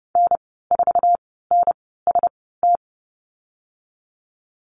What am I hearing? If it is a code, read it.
Morse code, heard as a keyed tone, about 30 wpm, spaced as for 17 wpm, 700 Hz. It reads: N4NST
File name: Lesson 14g - Number 2 - Call Signs 30wpm